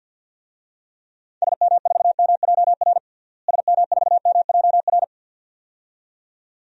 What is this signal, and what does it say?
Morse code, heard as a keyed tone, about 52 wpm, 700 Hz. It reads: SM4MJR SM4MJR
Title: DXCC-100 Call Signs - 6 Characters - Repeated 2x - 50wpm